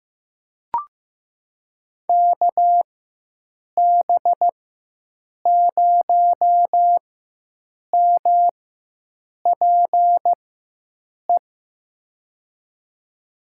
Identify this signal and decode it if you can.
Morse code, heard as a keyed tone, about 15 wpm, spaced as for 7 wpm, 700 Hz. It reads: KB0MPE
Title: USA Call Signs - 6 Characters - 4x Character Spacing - 15wpm